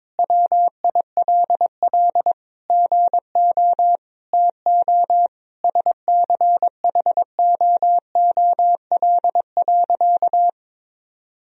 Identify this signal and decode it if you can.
Morse code, heard as a keyed tone, about 22 wpm, 700 Hz. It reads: WILL GO TO SCHOOL.
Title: Sentences from Top 1000 Words - Review Part 1 - 22wpm